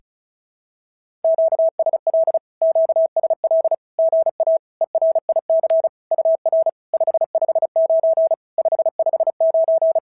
Morse code, heard as a keyed tone, about 34 wpm, 650 Hz. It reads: QSL QSL GA ERIC UR 559 559
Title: SOTA QSO Practice 35wpm